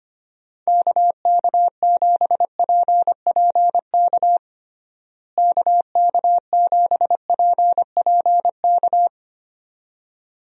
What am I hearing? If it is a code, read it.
Morse code, heard as a keyed tone, about 25 wpm, 700 Hz. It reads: KK7PPK KK7PPK
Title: USA Call Signs - 6 Characters - Repeated 2x - 25wpm